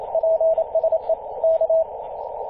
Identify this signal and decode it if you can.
Morse code, heard as a keyed tone, about 28 wpm, 650 Hz, with radio noise. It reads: PSE K
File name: 8q7ca-21-013mhz-cw